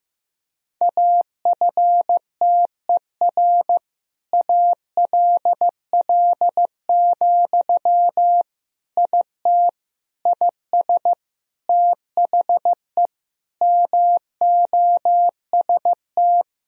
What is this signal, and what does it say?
Morse code, heard as a keyed tone, about 15 wpm, 700 Hz. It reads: AFTER ALL, IT IS THE MOST
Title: Sentences from Top 1000 Words - Review Part 4 - 15wpm